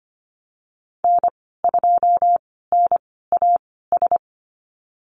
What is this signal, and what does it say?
Morse code, heard as a keyed tone, about 25 wpm, spaced as for 16 wpm, 700 Hz. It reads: N2NAS
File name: Lesson 15g - Number 5 - Call Signs 25wpm